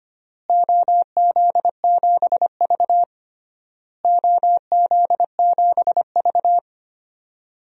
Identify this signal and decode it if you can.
Morse code, heard as a keyed tone, about 25 wpm, 700 Hz. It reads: OZ7V OZ7V
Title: DXCC-100 Call Signs - 4 Characters - Repeated 3x - 25wpm